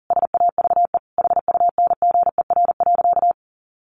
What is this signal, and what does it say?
Morse code, heard as a keyed tone, about 40 wpm, 700 Hz. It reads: SAVE HUNGER.
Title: A Sound Of Thunder 40wpm